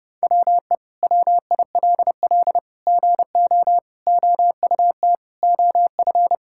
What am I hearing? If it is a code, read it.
Morse code, heard as a keyed tone, about 30 wpm, 700 Hz. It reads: WE WILL GO OUT OF
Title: Sentences from Top 1000 Words - Review Part 6 - 30wpm